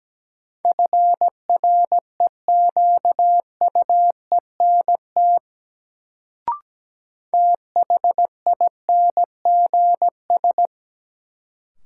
Morse code, heard as a keyed tone, about 17 wpm, 700 Hz. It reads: FREQUENT THINGS
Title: Sentences from Top 1400 Words - New Word Drill Alt - 17wpm